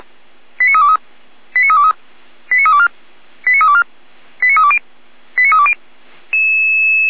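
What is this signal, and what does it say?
Five-tone selective call (ZVEI), heard as quick runs of five tones, tones of about 70 ms. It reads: 89323 89323 89324 89324 89320 89320
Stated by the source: Five-tone ZVEI selective call received by SDR.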